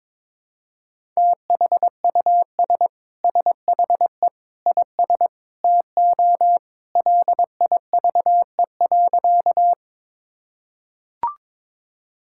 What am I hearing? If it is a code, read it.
Morse code, heard as a keyed tone, about 21 wpm, 700 Hz. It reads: THUS SHE IS TO LIVE.
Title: Sentences from Top 1000 Words - Review Part 5 - 22wpm